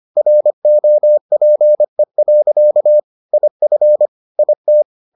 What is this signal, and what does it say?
Morse code, heard as a keyed tone, about 25 wpm, 600 Hz. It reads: ROPE. IF IT